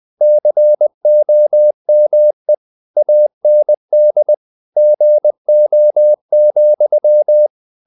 Morse code, heard as a keyed tone, about 20 wpm, 600 Hz. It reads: COME AND GO,
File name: CW / Practice / EnglishQuotations0320